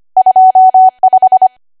Morse code, heard as a keyed tone, about 25 wpm, 750 Hz. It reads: 25
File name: cwa / W1AW-Level3 / 25WPM